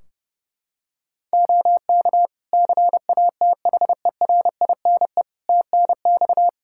Morse code, heard as a keyed tone, about 30 wpm, 700 Hz. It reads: OK CATHERINE TNX
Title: Practice QSO - Exchange 4 - Ending - 30wpm